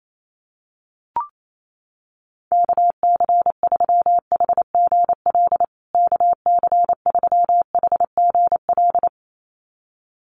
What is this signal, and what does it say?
Morse code, heard as a keyed tone, about 28 wpm, 700 Hz. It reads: KC3HGL KC3HGL
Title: USA Call Signs Repeated 2x 28wpm